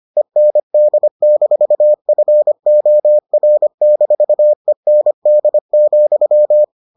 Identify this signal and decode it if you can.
Morse code, heard as a keyed tone, about 25 wpm, 600 Hz. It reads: END-FOR-END,